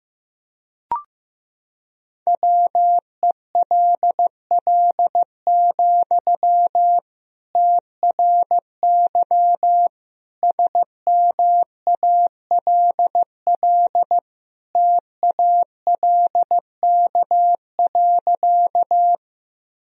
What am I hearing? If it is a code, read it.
Morse code, heard as a keyed tone, about 16 wpm, 700 Hz. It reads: WELL, TRY SMALL TALK.
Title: Sentences from Top 1000 Words - Review Part 1 - 15wpm